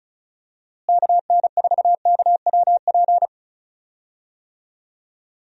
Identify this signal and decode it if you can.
Morse code, heard as a keyed tone, about 36 wpm, 700 Hz. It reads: KN4KWP
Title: USA Call Signs - 6 Characters - Repeated 2x - 35wpm